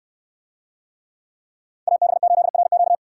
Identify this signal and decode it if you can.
Morse code, heard as a keyed tone, about 68 wpm, 700 Hz. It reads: AB8RQ